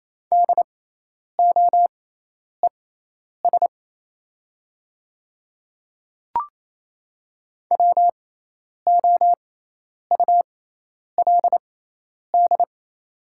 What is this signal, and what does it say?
Morse code, heard as a keyed tone, about 28 wpm, spaced as for 10 wpm, 700 Hz. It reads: DOES WOULD
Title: 6x Character Spacing - Top 300 Words - 28wpm